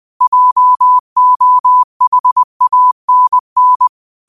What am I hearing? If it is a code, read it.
Morse code, heard as a keyed tone, about 20 wpm, 1000 Hz. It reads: JOHANN
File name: qotdfiles / 20wpm